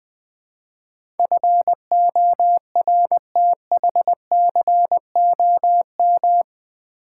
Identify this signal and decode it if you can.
Morse code, heard as a keyed tone, about 20 wpm, 700 Hz. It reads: FORTHCOM
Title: Build A Word - 4 Letters At A Time - 11-Letter Words - 20wpm